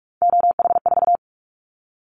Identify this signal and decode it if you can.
Morse code, heard as a keyed tone, about 44 wpm, 700 Hz. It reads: OH4
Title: Callsign Prefixes for Top 50 DXCC - 45wpm